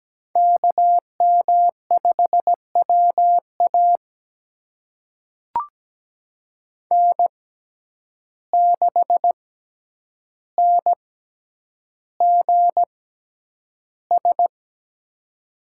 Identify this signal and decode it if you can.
Morse code, heard as a keyed tone, about 17 wpm, spaced as for 9 wpm, 700 Hz. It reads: KM5WA N6NGS
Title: USA Call Signs - 5 Characters - 6x Character Spacing - 17wpm